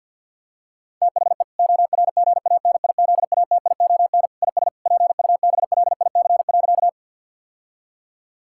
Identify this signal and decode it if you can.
Morse code, heard as a keyed tone, about 50 wpm, 700 Hz. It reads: THE ORGANIZATION IS PUBLIC.
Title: Sentences from Top 1000 Words - Review Part 5 - 50wpm